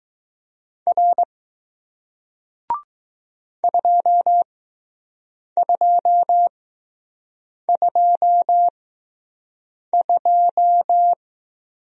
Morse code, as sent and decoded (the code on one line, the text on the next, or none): .-. / ..--- ..--- ..--- ..---
R 2222